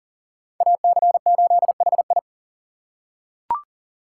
Morse code, heard as a keyed tone, about 40 wpm, 700 Hz. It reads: AC8HI
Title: USA Call Signs - 3x Character Spacing - 40wpm